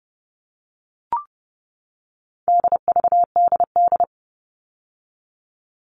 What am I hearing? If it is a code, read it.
Morse code, heard as a keyed tone, about 30 wpm, 700 Hz. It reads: DVDD
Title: All Cut Numbers - 4-Digit 30wpm